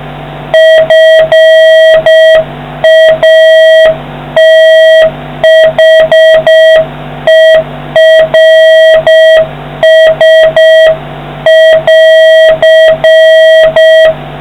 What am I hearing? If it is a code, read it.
Morse code, heard as a keyed tone, about 6 wpm, 1950 Hz, with noise. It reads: FATHERS<AR>